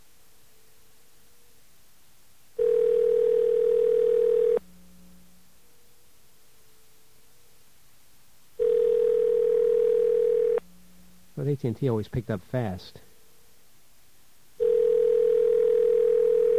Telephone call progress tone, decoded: ringback tone